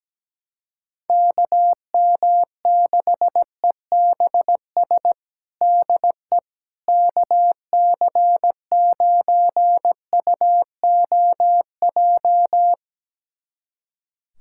Morse code, heard as a keyed tone, about 17 wpm, 700 Hz. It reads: KM6EBS DE KC9UOJ
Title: USA Callsign DE Callsign - 6 Characters - 17wpm